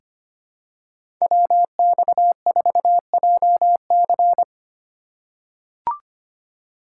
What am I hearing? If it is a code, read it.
Morse code, heard as a keyed tone, about 25 wpm, 700 Hz. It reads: WX4JC
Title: USA Call Signs - 5 Characters - 6x Character Spacing - 25wpm